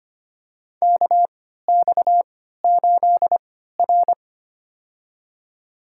Morse code, heard as a keyed tone, about 25 wpm, spaced as for 14 wpm, 700 Hz. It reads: KX8R
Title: USA Call Signs - 4 Characters - 3x Character Spacing - 25wpm